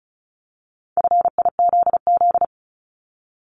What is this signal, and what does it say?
Morse code, heard as a keyed tone, about 36 wpm, 700 Hz. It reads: FIZZ
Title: Multi-letter Phonemes Repeated 3x 35wpm